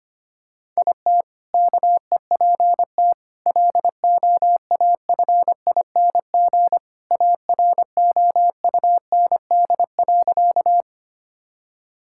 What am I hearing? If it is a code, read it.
Morse code, heard as a keyed tone, about 25 wpm, 700 Hz. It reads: IT KEPT LOAFING AROUND.
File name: practice / 25wpm / All-Age Appropriate Jokes - 25wpm